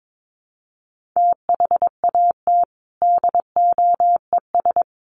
Morse code, heard as a keyed tone, about 22 wpm, 700 Hz. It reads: THAT DOES
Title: Sets of 4 Words 22wpm